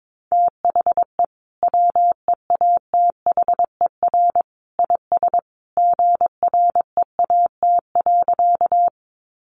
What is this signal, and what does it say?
Morse code, heard as a keyed tone, about 22 wpm, 700 Hz. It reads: THE WEATHER IS GREAT.
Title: Sentences from Top 800 Words 22wpm